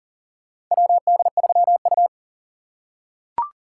Morse code, heard as a keyed tone, about 40 wpm, 700 Hz. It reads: WD3U
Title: USA Call Signs - 4 Characters - 3x Character Spacing - 40wpm